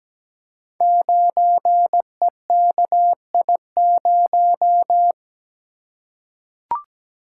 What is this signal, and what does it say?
Morse code, heard as a keyed tone, about 17 wpm, 700 Hz. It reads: 9EKI0
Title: Code Groups - 5 Letter:Numbers - 17wpm